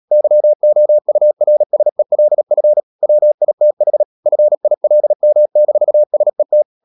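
Morse code, heard as a keyed tone, about 37 wpm, 600 Hz. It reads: YOURSELF WITH FILM-SET
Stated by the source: Morse code practice at 35–39 wpm sending English text of Ken Follett's novel Never, from CW med Gnister.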